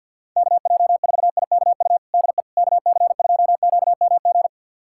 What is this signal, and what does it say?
Morse code, heard as a keyed tone, about 46 wpm, 690 Hz. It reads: KJ4IKU DE KC2QMG
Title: USA Callsign DE Callsign - 6 Characters - 50wpm